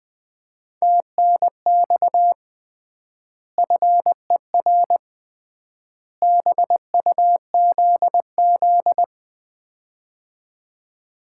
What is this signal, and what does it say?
Morse code, heard as a keyed tone, about 20 wpm, 700 Hz. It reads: TNX FER BUZZ
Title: Common QSO Phrases - 3x Word-Spacing - 20wpm